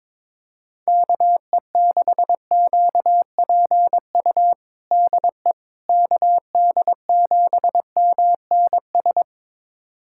Morse code, heard as a keyed tone, about 21 wpm, 700 Hz. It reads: KE6QPU DE KD7MNS
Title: USA Callsign DE Callsign 22wpm